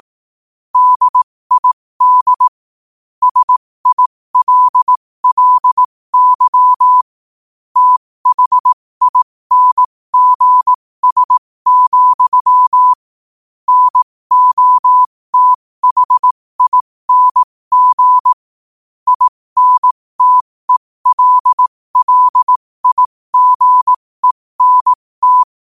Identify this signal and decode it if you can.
Morse code, heard as a keyed tone, about 18 wpm, spaced as for 15 wpm, 1000 Hz. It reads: DID SILLY THINGS, NOTHING INTELLIGENT
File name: qotdfiles / 15wpm